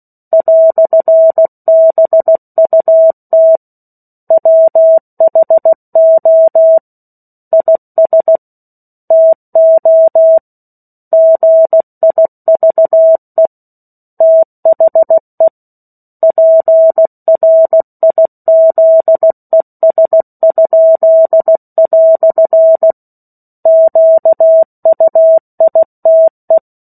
Morse code, heard as a keyed tone, about 16 wpm, 650 Hz. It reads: "BUT WHO IS TO GIVE THE PRIZES?" QUITE
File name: 03-Chapter-03 16 WPM